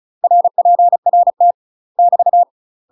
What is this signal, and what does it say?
Morse code, heard as a keyed tone, about 34 wpm, 700 Hz. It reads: RPRT <BT>
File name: UK MASTER QSOs TRAINING / 35WPM / UK MASTER QSOs 0004